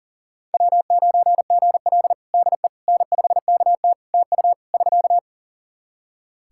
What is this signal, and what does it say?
Morse code, heard as a keyed tone, about 40 wpm, 700 Hz. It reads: W9GL DE N5KT TU <SK>
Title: Practice QSO - Exchange 4 - Ending - 40wpm